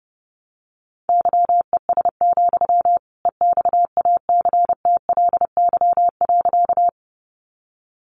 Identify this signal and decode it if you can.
Morse code, heard as a keyed tone, about 30 wpm, 700 Hz. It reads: YES, EXACTLY.
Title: Sentences from Top 900 Words 30wpm